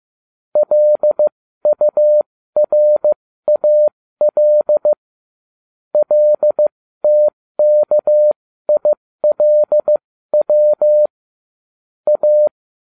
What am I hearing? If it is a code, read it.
Morse code, heard as a keyed tone, about 15 wpm, spaced as for 13 wpm, 600 Hz. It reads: LURAL LTKILW A